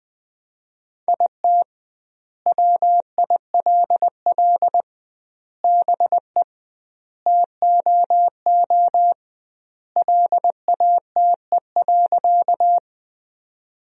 Morse code, heard as a keyed tone, about 20 wpm, 700 Hz. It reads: IT WILL BE TOO LATE.
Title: 2x Word-Spacing - Sentences from Top 300 Words - 20wpm